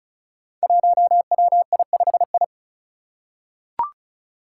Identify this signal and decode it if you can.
Morse code, heard as a keyed tone, about 36 wpm, 700 Hz. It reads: 1WI5I
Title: Five Challenging Letter-Number Combinations 35wpm